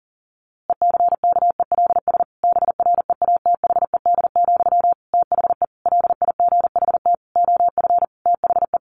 Morse code, heard as a keyed tone, about 40 wpm, 700 Hz. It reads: ECKELS BREATHED, THE LIGHT OF THE
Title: A Sound Of Thunder 40wpm